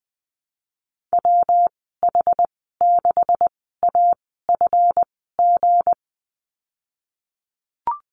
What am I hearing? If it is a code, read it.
Morse code, heard as a keyed tone, about 20 wpm, spaced as for 15 wpm, 700 Hz. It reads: WH6AFG
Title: 173 Lesson 28g - Number 9 - Call Signs 20wpm